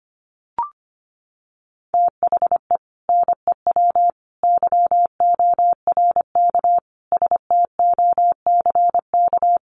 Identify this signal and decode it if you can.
Morse code, heard as a keyed tone, about 25 wpm, 700 Hz. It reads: THE NEW YORK STOCK
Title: Sets of 5 Words 25wpm